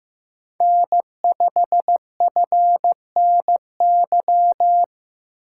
Morse code, heard as a keyed tone, about 15 wpm, 700 Hz. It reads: N5FNY
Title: USA Call Signs - 5 Characters - 8x Character Spacing - 15wpm